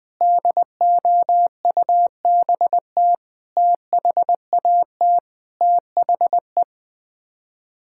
Morse code, heard as a keyed tone, about 21 wpm, 700 Hz. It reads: DOUBT THAT THE
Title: Sets of 6 Words - 20wpm